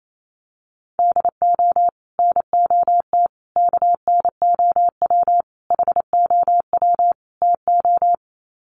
Morse code, heard as a keyed tone, about 28 wpm, 700 Hz. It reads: DO NOT KNOW HOW TO
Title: Sets of 5 Words 28wpm